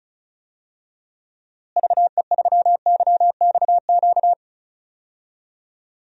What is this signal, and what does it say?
Morse code, heard as a keyed tone, about 34 wpm, 700 Hz. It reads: VE3YXQ
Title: DXCC-100 Call Signs - 6 Characters - Repeated 3x - 35wpm